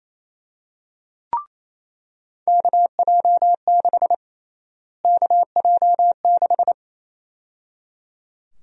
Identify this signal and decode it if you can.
Morse code, heard as a keyed tone, about 27 wpm, 700 Hz. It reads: KJ6 KJ6
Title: USA Call Sign Prefixes - Repeated 2x - 28wpm